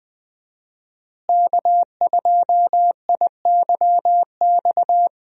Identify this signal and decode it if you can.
Morse code, heard as a keyed tone, about 20 wpm, 700 Hz. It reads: K2IYX
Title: USA Call Signs - 5 Characters - Repeated 2x - 20wpm